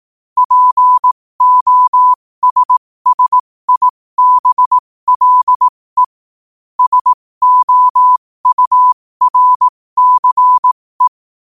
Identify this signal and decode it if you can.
Morse code, heard as a keyed tone, about 18 wpm, spaced as for 15 wpm, 1000 Hz. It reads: POSSIBLE SOURCE
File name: qotdfiles / 15wpm